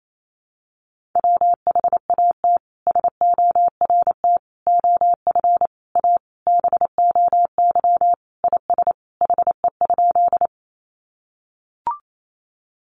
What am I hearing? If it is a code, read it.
Morse code, heard as a keyed tone, about 28 wpm, 700 Hz. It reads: WHAT SORT OF A BOY IS HE?
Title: Sentences from Top 1300 Words 28wpm